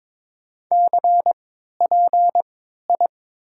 Morse code, heard as a keyed tone, about 21 wpm, 700 Hz. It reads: C P I
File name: Top 90 QSO Elements - 3x Character Spacing - 22wpm